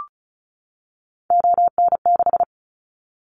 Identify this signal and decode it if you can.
Morse code, heard as a keyed tone, about 34 wpm, 700 Hz. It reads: ON6
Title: Callsign Prefixes for Top 100 DXCC - 35wpm